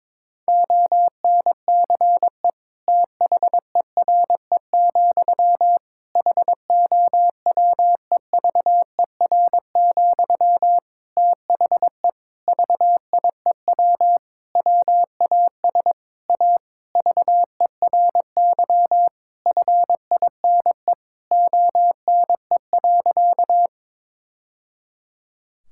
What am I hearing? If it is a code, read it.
Morse code, heard as a keyed tone, about 21 wpm, 700 Hz. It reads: ONCE THERE, HOWEVER, THE VIEW WAS A VERY FINE ONE.
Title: Sentences from Top 1000 Words - Review Part 6 - 22wpm